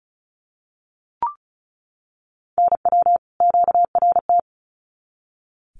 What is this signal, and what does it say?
Morse code, heard as a keyed tone, about 34 wpm, 700 Hz. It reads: NW QRT
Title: Common QSO Phrases 35wpm